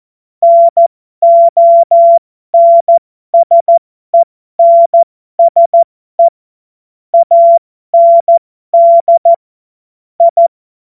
Morse code, heard as a keyed tone, about 14 wpm, spaced as for 12 wpm, 680 Hz. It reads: NONSENSE AND I
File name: Morsekurs / Witze